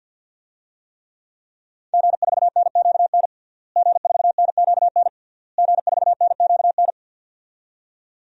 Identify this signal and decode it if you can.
Morse code, heard as a keyed tone, about 52 wpm, 700 Hz. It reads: G4NQN G4NQN G4NQN
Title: DXCC-100 Call Signs - 5 Characters - Repeated 3x - 50wpm